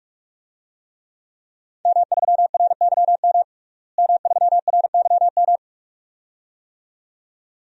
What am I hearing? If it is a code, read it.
Morse code, heard as a keyed tone, about 46 wpm, 700 Hz. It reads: M3RYM M3RYM
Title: DXCC-100 Call Signs - 5 Characters - Repeated 2x - 45wpm